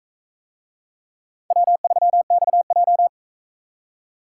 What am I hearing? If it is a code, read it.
Morse code, heard as a keyed tone, about 42 wpm, 700 Hz. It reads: W3XJ